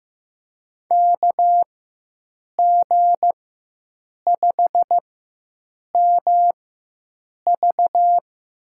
Morse code, heard as a keyed tone, about 15 wpm, spaced as for 7 wpm, 700 Hz. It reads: KG5MV
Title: USA Call Signs - 4x Character Spacing - 15wpm